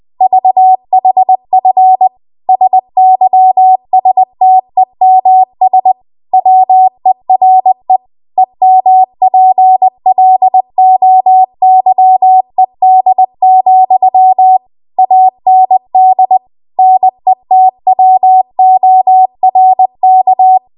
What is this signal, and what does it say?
Morse code, heard as a keyed tone, about 20 wpm, 750 Hz. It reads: VHF SYSTEMS WERE EMPLOYED, AND NETWORK
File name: Archive / 20 WPM / 20WPM